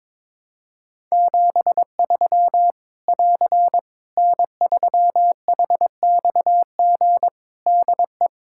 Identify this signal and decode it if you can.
Morse code, heard as a keyed tone, about 22 wpm, 700 Hz. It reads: 73 <AR> N3HXG DE
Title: K1USN SST QSO Practice 22wpm